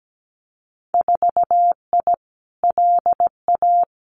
Morse code, heard as a keyed tone, about 17 wpm, 700 Hz. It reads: 4I LA
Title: Winter Field Day Exchanges Encore 17wpm